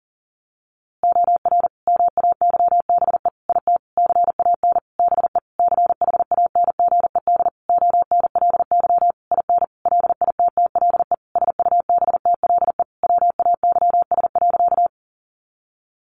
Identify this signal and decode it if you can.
Morse code, heard as a keyed tone, about 40 wpm, 700 Hz. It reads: OR MAYBE IT CAN BE CHANGED ONLY IN LITTLE SUBTLE WAYS.
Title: A Sound Of Thunder 40wpm